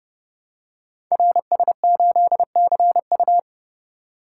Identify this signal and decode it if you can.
Morse code, heard as a keyed tone, about 30 wpm, 700 Hz. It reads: RS8CU
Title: Code Groups - 5 Letter:Numbers - 30wpm